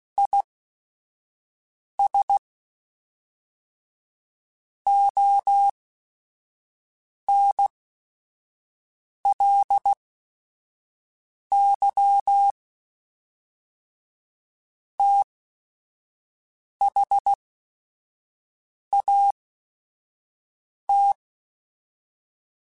Morse code, heard as a keyed tone, about 16 wpm, spaced as for 5 wpm, 800 Hz. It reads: IS ONLY THAT
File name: Technical Subjects / Technical Subjects-mp3s / AND Gate @5